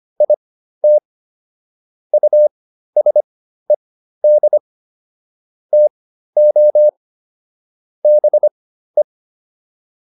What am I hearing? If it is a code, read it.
Morse code, heard as a keyed tone, about 25 wpm, spaced as for 13 wpm, 600 Hz. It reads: IT USED TO BE